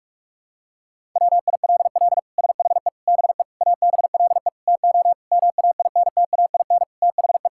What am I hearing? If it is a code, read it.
Morse code, heard as a keyed tone, about 41 wpm, 700 Hz. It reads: WILL SHE BE ABLE TO MAINTAIN THE
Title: Sentences from Top 1000 Words - Review Part 5 - 45wpm